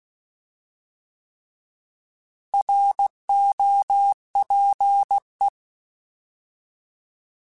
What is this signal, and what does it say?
Morse code, heard as a keyed tone, about 16 wpm, 800 Hz. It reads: ROPE
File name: Alphabetical / Alphabetical mp3s / Rs @15